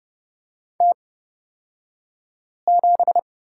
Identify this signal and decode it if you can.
Morse code, heard as a keyed tone, about 30 wpm, spaced as for 5 wpm, 700 Hz. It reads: T7
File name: Lesson 37d - Letter x - Two Characters 30wpm